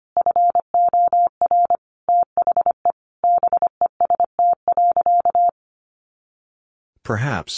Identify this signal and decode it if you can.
Morse code, heard as a keyed tone, about 25 wpm, 700 Hz. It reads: FOR THE BEST.
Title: Sentences from Top 500 Words 25wpm